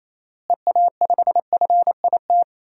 Morse code, heard as a keyed tone, about 28 wpm, 700 Hz. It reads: EA5FIT
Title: DXCC-100 Call Signs - 6 Characters - Repeated 3x - 28wpm